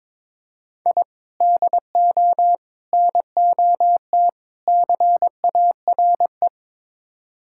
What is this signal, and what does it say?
Morse code, heard as a keyed tone, about 22 wpm, 700 Hz. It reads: I DO NOT CARE
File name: practice / 22wpm / Build A Phrase - 5-Word Phrases - 22wpm